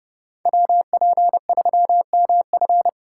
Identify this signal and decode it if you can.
Morse code, heard as a keyed tone, about 37 wpm, 700 Hz. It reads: WP3MF
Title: Speed Racing - USA 2x2 Call Signs 30wpm